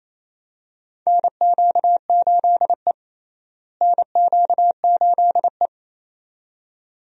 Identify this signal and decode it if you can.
Morse code, heard as a keyed tone, about 28 wpm, 700 Hz. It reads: NQ8E NQ8E
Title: USA Call Signs - 4 Characters - Repeated 2x - 28wpm